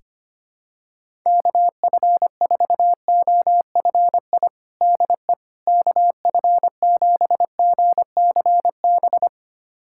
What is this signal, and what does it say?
Morse code, heard as a keyed tone, about 25 wpm, 700 Hz. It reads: KF4OFI DE KF7GCB
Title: USA Callsign DE Callsign - 6 Characters - 25wpm